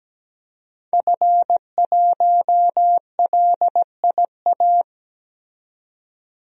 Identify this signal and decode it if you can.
Morse code, heard as a keyed tone, about 17 wpm, 700 Hz. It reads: F1LIA
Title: DXCC-100 Call Signs - 5 Characters - 17wpm